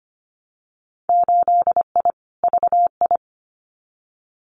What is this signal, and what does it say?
Morse code, heard as a keyed tone, about 25 wpm, 700 Hz. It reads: 8I VI
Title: Winter Field Day Exchanges Encore 25wpm